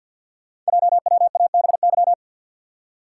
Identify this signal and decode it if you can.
Morse code, heard as a keyed tone, about 52 wpm, 700 Hz. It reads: JWABY
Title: Code Groups - 5 Letters - 50wpm